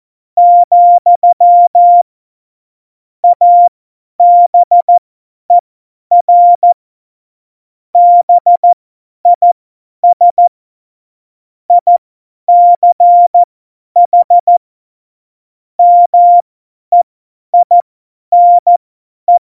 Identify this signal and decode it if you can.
Morse code, heard as a keyed tone, about 14 wpm, spaced as for 10 wpm, 700 Hz. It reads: , ABER BIS ICH MEINE
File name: Morsekurs / Witze / Witz 22. Mai 2018, 50 F70